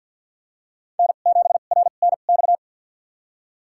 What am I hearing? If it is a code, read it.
Morse code, heard as a keyed tone, about 50 wpm, spaced as for 36 wpm, 690 Hz. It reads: N7RNX
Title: USA Call Signs - 5 Characters - 2x Character Spacing - 50wpm